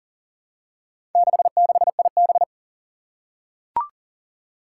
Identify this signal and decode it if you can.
Morse code, heard as a keyed tone, about 39 wpm, 700 Hz. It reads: 66IB
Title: Four Challenging Letter-Number Combinations 40wpm